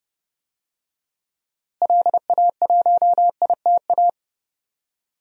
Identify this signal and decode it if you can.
Morse code, heard as a keyed tone, about 28 wpm, 700 Hz. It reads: LA1ITA
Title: DXCC-100 Call Signs - 6 Characters - Repeated 3x - 30wpm